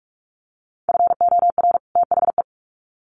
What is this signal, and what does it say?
Morse code, heard as a keyed tone, about 44 wpm, 700 Hz. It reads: FOR THE
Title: Sets of 2 Words 45wpm